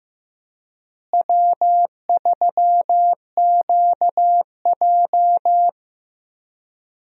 Morse code, heard as a keyed tone, about 15 wpm, 700 Hz. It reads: W3QJ
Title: USA Call Signs - 4 Characters - Repeated 3x - 15wpm